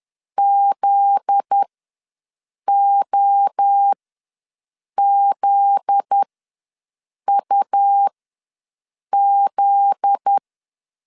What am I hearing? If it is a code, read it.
Morse code, heard as a keyed tone, about 10 wpm, spaced as for 6 wpm, 800 Hz. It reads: ZOZUZ
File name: LEZIONI / 7 GF9Z / 7 GF9Z ws 10wpm 3919